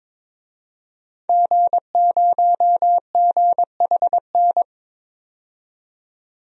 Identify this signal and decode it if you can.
Morse code, heard as a keyed tone, about 21 wpm, 700 Hz. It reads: G0GHN
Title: DXCC-100 Call Signs - 5 Characters - 22wpm